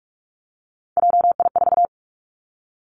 Morse code, heard as a keyed tone, about 46 wpm, 700 Hz. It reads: JI4
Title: Three Challenging Letter-Number Combinations 45wpm